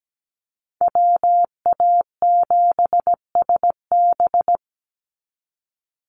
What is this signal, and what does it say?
Morse code, heard as a keyed tone, about 17 wpm, 700 Hz. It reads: WA7SB
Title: NA SOTA Chaser Callsigns 17wpm